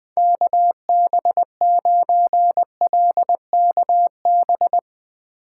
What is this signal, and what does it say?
Morse code, heard as a keyed tone, about 20 wpm, 700 Hz. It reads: KB9LKB
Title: USA Call Signs - 6 Characters - 8x Character Spacing - 20wpm